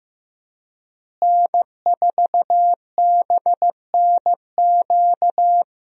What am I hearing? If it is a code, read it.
Morse code, heard as a keyed tone, about 16 wpm, 700 Hz. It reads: N4BNQ
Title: USA Call Signs - 5 Characters - Repeated 2x - 15wpm